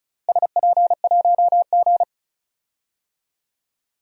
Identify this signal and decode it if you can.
Morse code, heard as a keyed tone, about 36 wpm, 700 Hz. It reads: SP1G
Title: DXCC-100 Call Signs - 4 Characters - Repeated 2x - 35wpm